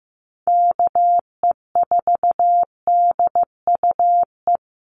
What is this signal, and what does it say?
Morse code, heard as a keyed tone, about 15 wpm, 700 Hz. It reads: KE4DUE
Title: USA Call Signs Repeated 2x 15wpm